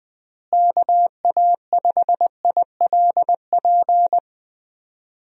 Morse code, heard as a keyed tone, about 20 wpm, 700 Hz. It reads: KA5ILP
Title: USA Call Signs - 6 Characters - Repeated 2x - 20wpm